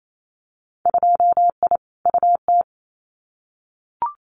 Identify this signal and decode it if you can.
Morse code, heard as a keyed tone, about 28 wpm, 700 Hz. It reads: 2I UT
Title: Winter Field Day Exchanges 28wpm